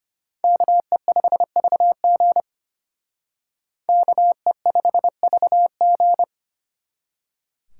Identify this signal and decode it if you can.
Morse code, heard as a keyed tone, about 32 wpm, 700 Hz. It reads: KE5VG KE5VG